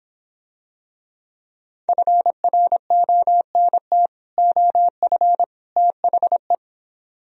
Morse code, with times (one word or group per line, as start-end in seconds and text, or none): 1.9-4.1 FRONT
4.4-5.4 OF
5.8-6.5 THE